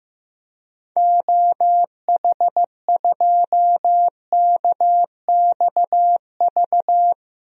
Morse code, heard as a keyed tone, about 15 wpm, 700 Hz. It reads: OH2KXV
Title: DXCC-100 Call Signs - 6 Characters - 15wpm